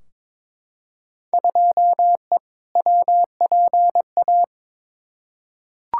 Morse code, heard as a keyed tone, about 22 wpm, 700 Hz. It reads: 2E WPA
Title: Field Day Exchanges Encore - 22wpm